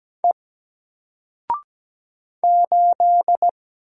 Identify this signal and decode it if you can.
Morse code, heard as a keyed tone, about 17 wpm, 700 Hz. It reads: E 8
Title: Single Character (Letter:Number:Punctuation:Prosign) Alt 17wpm